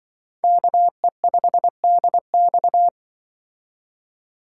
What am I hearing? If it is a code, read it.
Morse code, heard as a keyed tone, about 24 wpm, 700 Hz. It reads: KE5DX
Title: Speed Racing - USA 2x2 Call Signs 20wpm